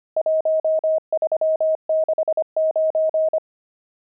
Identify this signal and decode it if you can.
Morse code, heard as a keyed tone, about 25 wpm, 630 Hz. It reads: 1369
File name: pangram-fs-25-cs-25-nums11